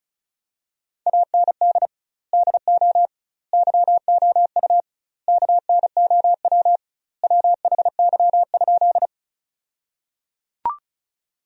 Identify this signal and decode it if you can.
Morse code, heard as a keyed tone, about 34 wpm, 700 Hz. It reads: AND DO YOU KNOW WHY?
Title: 2x Word-Spacing - Sentences from Top 200 Words - 35wpm